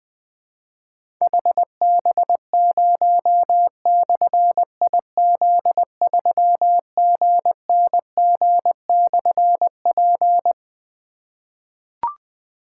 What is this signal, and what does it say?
Morse code, heard as a keyed tone, about 20 wpm, 700 Hz. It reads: HB0/IZ3GNG/P
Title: Difficult Call Signs 20wpm